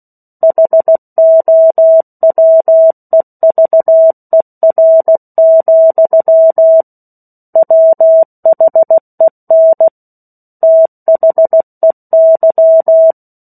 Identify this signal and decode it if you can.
Morse code, heard as a keyed tone, about 16 wpm, 650 Hz. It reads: HOWEVER, WHEN THEY
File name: 03-Chapter-03 16 WPM